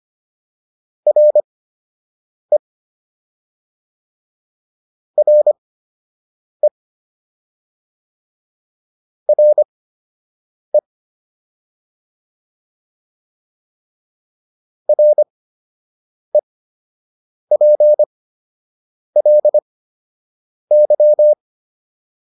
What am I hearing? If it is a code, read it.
Morse code, heard as a keyed tone, about 25 wpm, spaced as for 7 wpm, 600 Hz. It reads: RE RE RE REPLY